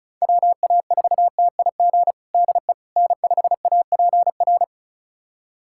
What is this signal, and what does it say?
Morse code, heard as a keyed tone, about 36 wpm, 700 Hz. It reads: WA4TIG DE N5APR
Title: USA Callsign DE Callsign 35wpm